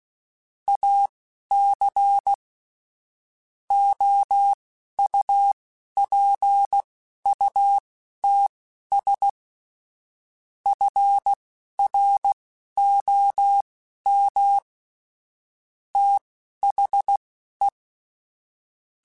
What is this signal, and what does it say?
Morse code, heard as a keyed tone, about 16 wpm, spaced as for 11 wpm, 800 Hz. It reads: AC OUPUTS FROM THE
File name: Technical Subjects / Technical Subjects-mp3s / Sw Pwr Sup @10